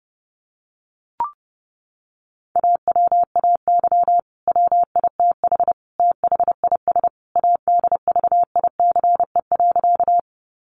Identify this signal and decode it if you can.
Morse code, heard as a keyed tone, about 30 wpm, 700 Hz. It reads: AWAY WITH THIS ADVICE.
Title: Sentences from Top 1100 Words 30wpm